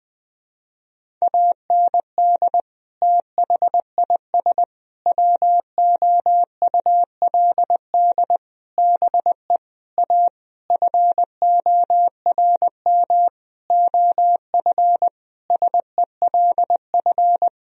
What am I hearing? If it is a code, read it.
Morse code, heard as a keyed tone, about 20 wpm, 700 Hz. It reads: AND THIS WOULD BE A FORM OF SELF
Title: Sentences from Top 1000 Words - Review Part 1 - 20wpm